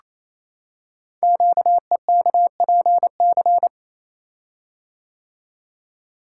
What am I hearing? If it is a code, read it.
Morse code, heard as a keyed tone, about 29 wpm, 700 Hz. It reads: QEKPC
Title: Code Groups - 5 Letters - 28wpm